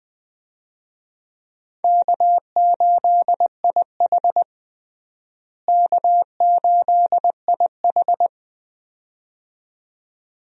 Morse code, heard as a keyed tone, about 21 wpm, 700 Hz. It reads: K8IH K8IH
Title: USA Call Signs - 4 Characters - Repeated 2x - 20wpm